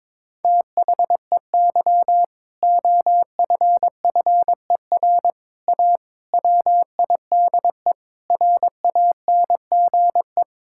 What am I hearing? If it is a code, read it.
Morse code, heard as a keyed tone, about 22 wpm, 700 Hz. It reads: THEY OFFER A WIDE RANGE
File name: practice / 22wpm / Sentences with 4-Word Phrases - 22wpm